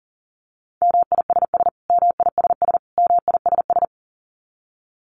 Morse code, heard as a keyed tone, about 40 wpm, 700 Hz. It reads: MISS MISS MISS
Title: Top 500 Words Repeated 40wpm